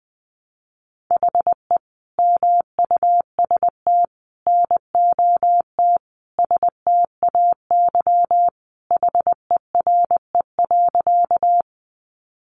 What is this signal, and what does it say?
Morse code, heard as a keyed tone, about 20 wpm, 700 Hz. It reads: HE MUST NOT STAY HERE.
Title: Sentences from Top 600 Words 20wpm